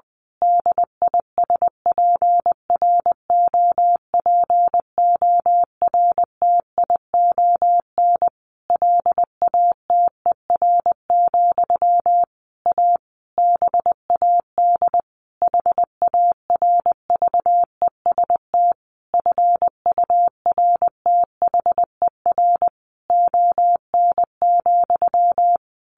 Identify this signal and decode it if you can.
Morse code, heard as a keyed tone, about 20 wpm, 700 Hz. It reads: DISPROPORTION LATER, A BAD HARVEST FURTHER ON,
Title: A Sound Of Thunder 20wpm